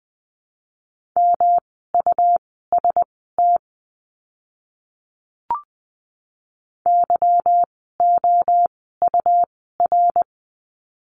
Morse code, heard as a keyed tone, about 20 wpm, spaced as for 15 wpm, 700 Hz. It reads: MUST YOUR
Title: 237 Lesson 38f - Letter q - Words 20wpm